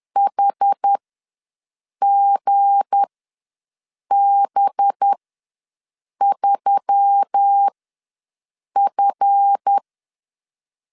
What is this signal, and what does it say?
Morse code, heard as a keyed tone, about 10 wpm, spaced as for 6 wpm, 800 Hz. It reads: HGB3F
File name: LEZIONI / 7 GF9Z / 7 GF ws 10wpm 3918